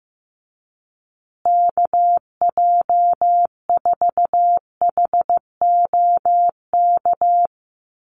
Morse code, heard as a keyed tone, about 15 wpm, 700 Hz. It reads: KJ4HOK
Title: USA Call Signs Repeated 2x 15wpm